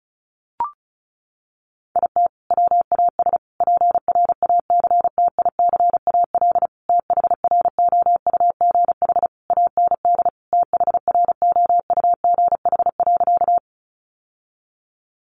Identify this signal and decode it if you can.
Morse code, heard as a keyed tone, about 34 wpm, 700 Hz. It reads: IT WAS PRACTICAL THROUGH AND THROUGH.
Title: Sentences from Top 1400 Words 35wpm